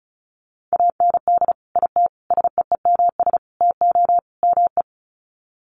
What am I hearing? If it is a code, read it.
Morse code, heard as a keyed tone, about 34 wpm, 700 Hz. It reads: AND IT SEEMS TO ME
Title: Sets of 5 Words Encore2 35wpm